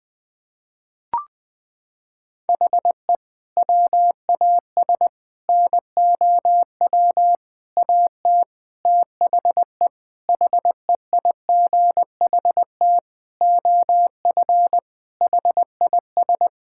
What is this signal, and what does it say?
Morse code, heard as a keyed tone, about 20 wpm, 700 Hz. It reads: HE WAS NOW AT THE HEIGHT OF HIS
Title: Sentences from Top 1600 Words 20wpm